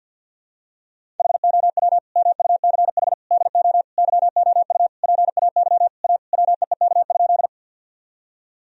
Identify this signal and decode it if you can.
Morse code, heard as a keyed tone, about 50 wpm, 700 Hz. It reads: HOW MUCH DO YOU PAY A WEEK?
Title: Sentences from Top 1000 Words - Review Part 2 - 50wpm